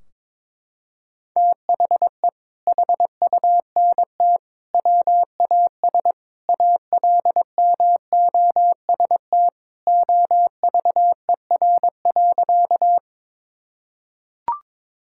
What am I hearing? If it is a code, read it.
Morse code, heard as a keyed tone, about 21 wpm, 700 Hz. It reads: THE HUNT WAS ALMOST OVER.
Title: Sentences from Top 1600 Words 22wpm